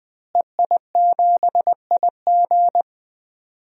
Morse code, heard as a keyed tone, about 20 wpm, 700 Hz. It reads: EI7IG
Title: DXCC-100 Call Signs - 5 Characters - Repeated 2x - 20wpm